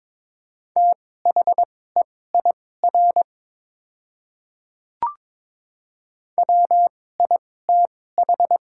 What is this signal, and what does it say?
Morse code, heard as a keyed tone, about 22 wpm, spaced as for 16 wpm, 700 Hz. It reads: THEIR WITH
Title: 2x Character Spacing - Top 100 Words - 22wpm